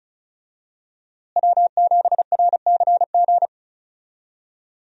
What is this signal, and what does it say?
Morse code, heard as a keyed tone, about 34 wpm, 700 Hz. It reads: W7RCG
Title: USA Call Signs - 5 Characters - Repeated 2x - 35wpm